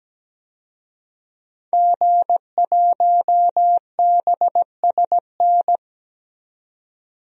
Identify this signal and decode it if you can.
Morse code, heard as a keyed tone, about 17 wpm, 700 Hz. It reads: G1BSN
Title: DXCC-100 Call Signs - 5 Characters - 17wpm